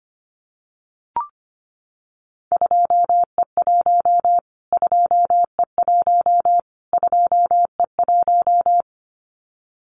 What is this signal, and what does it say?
Morse code, heard as a keyed tone, about 25 wpm, 700 Hz. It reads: 2E1 2E1 2E1
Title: Callsign Prefixes for Top 25 DXCC - Repeated 3x - 25wpm